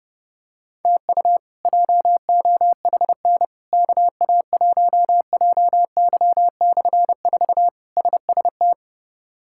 Morse code, heard as a keyed tone, about 30 wpm, 700 Hz. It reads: TU JOHN KA1JY/4 SST
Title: K1USN SST QSO Practice - 30wpm